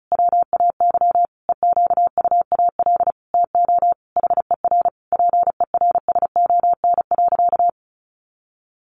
Morse code, heard as a keyed tone, about 34 wpm, 700 Hz. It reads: WAY EQUAL TO HER PERSON.
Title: Sentences from Top 1400 Words 35wpm